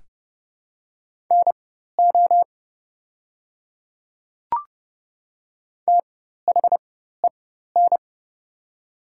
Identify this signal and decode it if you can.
Morse code, heard as a keyed tone, about 30 wpm, spaced as for 14 wpm, 700 Hz. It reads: NO THEN
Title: 4x Character Spacing - Top 100 Words - 30wpm